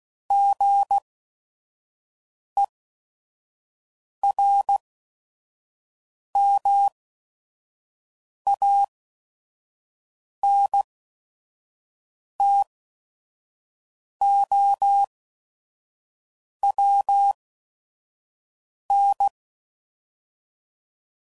Morse code, heard as a keyed tone, about 16 wpm, spaced as for 5 wpm, 800 Hz. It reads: GERMANTOWN
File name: CW Code Groups @5WPM / fromSM's / Cities